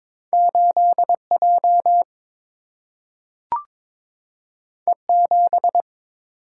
Morse code, heard as a keyed tone, about 22 wpm, 700 Hz. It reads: 8 J E 7
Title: Single-Letter Single-Number 22wpm